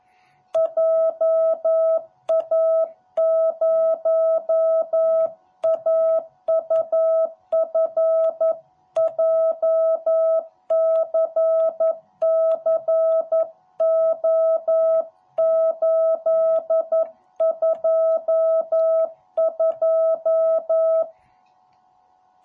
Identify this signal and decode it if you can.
Morse code, heard as a keyed tone, about 11 wpm, 660 Hz. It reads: JA0AUF JCCO822